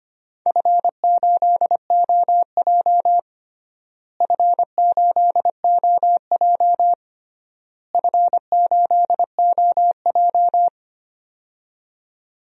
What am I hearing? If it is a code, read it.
Morse code, heard as a keyed tone, about 25 wpm, 700 Hz. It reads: F8OJ F8OJ F8OJ
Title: DXCC-100 Call Signs - 4 Characters - Repeated 3x - 25wpm